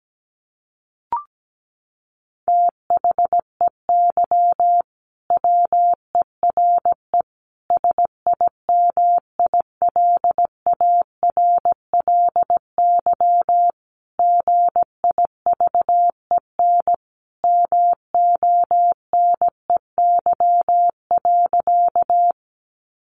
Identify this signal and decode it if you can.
Morse code, heard as a keyed tone, about 17 wpm, 700 Hz. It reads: THEY WERE SIMILARLY GIVEN MONEY.
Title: Sentences from Top 1400 Words 17wpm